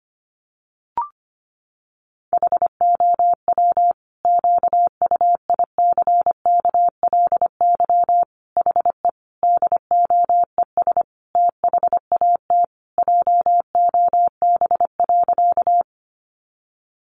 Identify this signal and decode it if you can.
Morse code, heard as a keyed tone, about 25 wpm, 700 Hz. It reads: HOW QUICKLY HE DOES THAT JOB.
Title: Sentences from Top 700 Words 25wpm